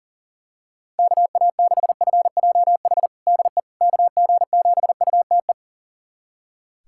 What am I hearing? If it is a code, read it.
Morse code, heard as a keyed tone, about 40 wpm, 690 Hz. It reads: KA6FJH DE KG7UTE
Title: USA Callsign DE Callsign - 6 Characters - 40wpm